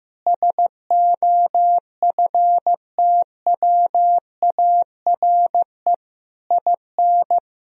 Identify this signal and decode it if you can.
Morse code, heard as a keyed tone, about 15 wpm, 700 Hz. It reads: SOFTWARE IN
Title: Sentences from Top 1000 Words - Review Part 3 - 15wpm